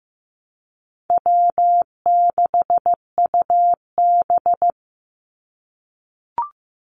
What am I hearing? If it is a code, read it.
Morse code, heard as a keyed tone, about 15 wpm, 700 Hz. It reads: W6UB
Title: NA SOTA Chaser Callsigns 15wpm